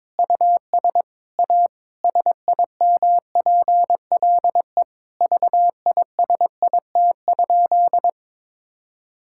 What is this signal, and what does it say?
Morse code, heard as a keyed tone, about 22 wpm, 700 Hz. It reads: US A SIMPLE VISIT?
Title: Sentences from Top 1000 Words - Review Part 6 - 22wpm